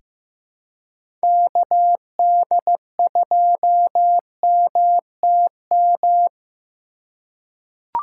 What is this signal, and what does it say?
Morse code, heard as a keyed tone, about 15 wpm, 700 Hz. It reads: KD2MTM
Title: USA Call Signs - 6 Characters - 2x Character Spacing - 15wpm